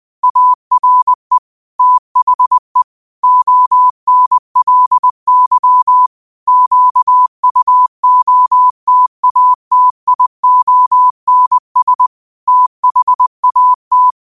Morse code, heard as a keyed tone, about 20 wpm, 1000 Hz. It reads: ARE THE ONLY QUOTATIONS THAT